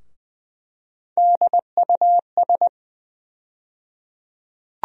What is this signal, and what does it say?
Morse code, heard as a keyed tone, about 20 wpm, 700 Hz. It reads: DUS
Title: USA Call Sign Postfixes - 3 Letters 20wpm